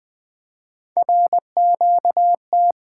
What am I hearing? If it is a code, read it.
Morse code, heard as a keyed tone, about 20 wpm, 700 Hz. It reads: RQT
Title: Three Letter-Number Combinations - 20wpm